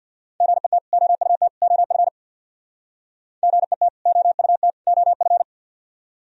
Morse code, heard as a keyed tone, about 55 wpm, 700 Hz. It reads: GET OUT OF GET OUT OF
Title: Speed-Racing - Sets of 3 Words 50wpm